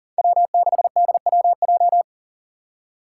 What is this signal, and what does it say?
Morse code, heard as a keyed tone, about 40 wpm, 700 Hz. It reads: W6DWJ
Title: Five Challenging Letter-Number Combinations 40wpm